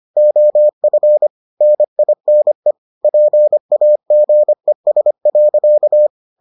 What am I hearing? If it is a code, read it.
Morse code, heard as a keyed tone, about 25 wpm, 600 Hz. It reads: OF NINE PAGES.